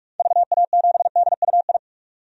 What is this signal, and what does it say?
Morse code, heard as a keyed tone, about 42 wpm, 700 Hz. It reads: VA7DUI
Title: DXCC-100 Call Signs - 6 Characters - 45wpm